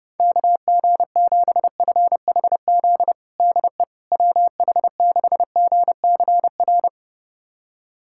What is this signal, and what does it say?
Morse code, heard as a keyed tone, about 32 wpm, 690 Hz. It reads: KG7FHZ DE WH6GCR
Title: USA Callsign DE Callsign - 6 Characters - 30wpm